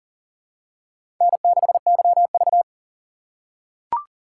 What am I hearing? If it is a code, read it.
Morse code, heard as a keyed tone, about 40 wpm, 700 Hz. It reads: N6YV
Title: USA Call Signs - 4 Characters - 6x Character Spacing - 40wpm